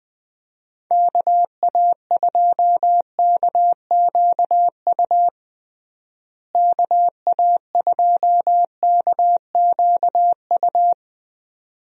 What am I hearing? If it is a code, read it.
Morse code, heard as a keyed tone, about 21 wpm, 700 Hz. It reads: KA2KQU KA2KQU
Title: USA Call Signs - 6 Characters - Repeated 2x - 20wpm